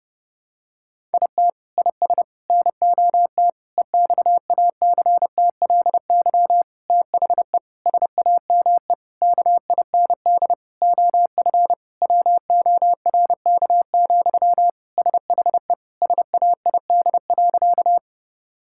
Morse code, heard as a keyed tone, about 30 wpm, 700 Hz. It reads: IT IS NOT EXACTLY THE SAME KIND OF WORK, SHE SAID.
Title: Sentences from Top 1000 Words - Review Part 6 - 30wpm